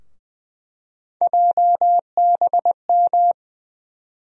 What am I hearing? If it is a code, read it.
Morse code, heard as a keyed tone, about 20 wpm, 700 Hz. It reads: JBM
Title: USA Call Sign Postfixes - 3 Letters 20wpm